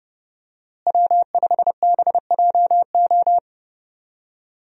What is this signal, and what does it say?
Morse code, heard as a keyed tone, about 30 wpm, 700 Hz. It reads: W5BJO
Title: USA Call Signs - 5 Characters - Repeated 3x - 30wpm